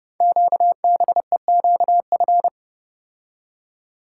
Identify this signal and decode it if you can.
Morse code, heard as a keyed tone, about 29 wpm, 700 Hz. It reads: QBEQF
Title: Code Groups - 5 Letters - 30wpm